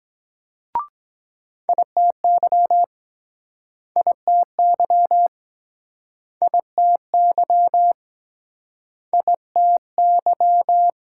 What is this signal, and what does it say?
Morse code, heard as a keyed tone, about 23 wpm, 700 Hz. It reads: ITY ITY ITY ITY